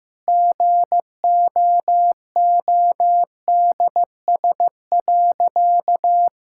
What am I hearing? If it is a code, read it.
Morse code, heard as a keyed tone, about 16 wpm, 700 Hz. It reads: GOODS.
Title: Sentences from Top 1000 Words - Review Part 6 - 15wpm